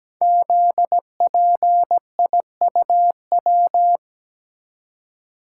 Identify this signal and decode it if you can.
Morse code, heard as a keyed tone, about 17 wpm, 700 Hz. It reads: ZPIUW
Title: Code Groups - 5 Letters - 17wpm